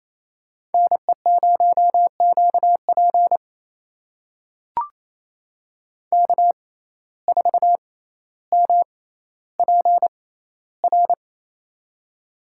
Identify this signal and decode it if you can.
Morse code, heard as a keyed tone, about 28 wpm, spaced as for 14 wpm, 700 Hz. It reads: NE0QP K4MPR
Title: USA Call Signs - 5 Characters - 6x Character Spacing - 28wpm